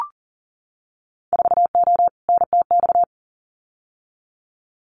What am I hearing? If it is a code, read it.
Morse code, heard as a keyed tone, about 37 wpm, 700 Hz. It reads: 4O NTX
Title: Winter Field Day Exchanges 40wpm